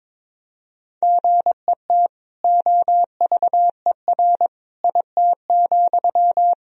Morse code, heard as a keyed tone, about 22 wpm, 700 Hz. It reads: GET OVER IT,
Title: Sentences from Top 1000 Words - Review Part 2 - 22wpm